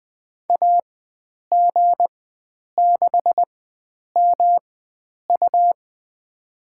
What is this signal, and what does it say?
Morse code, heard as a keyed tone, about 20 wpm, spaced as for 9 wpm, 700 Hz. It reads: AG6MU
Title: USA Call Signs - 5 Characters - 4x Character Spacing - 20wpm